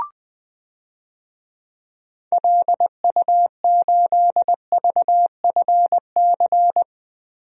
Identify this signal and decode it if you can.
Morse code, heard as a keyed tone, about 20 wpm, 700 Hz. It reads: LU8VFC
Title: DXCC-100 Call Signs - 6 Characters - Repeated 3x - 20wpm